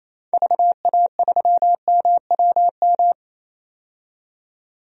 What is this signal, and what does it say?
Morse code, heard as a keyed tone, about 28 wpm, 700 Hz. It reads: VA3MWM
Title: DXCC-100 Call Signs - 6 Characters - 28wpm